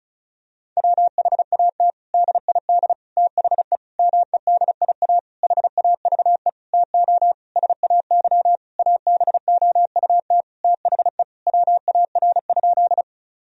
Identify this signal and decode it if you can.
Morse code, heard as a keyed tone, about 34 wpm, 700 Hz. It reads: WHAT DID THE MEDIA HAVE TO SAY ABOUT THE WAR?
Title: Sentences from Top 1000 Words - Review Part 4 - 35wpm